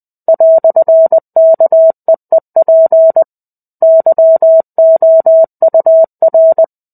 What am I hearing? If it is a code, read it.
Morse code, heard as a keyed tone, about 20 wpm, 650 Hz. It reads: "KEEP YOUR
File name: 05-Chapter-05 20 WPM